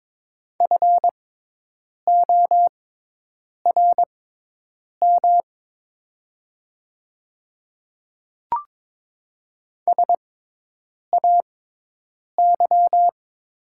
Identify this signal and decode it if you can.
Morse code, heard as a keyed tone, about 22 wpm, spaced as for 8 wpm, 700 Hz. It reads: FORM SAY
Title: 6x Character Spacing - Top 200 Words - 22wpm